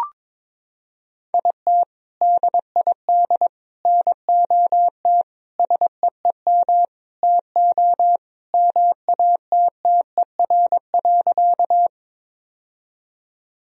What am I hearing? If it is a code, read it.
Morse code, heard as a keyed tone, about 22 wpm, 700 Hz. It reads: IT DID NOT SEEM TO MATTER.
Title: Sentences from Top 1000 Words - Review Part 1 - 22wpm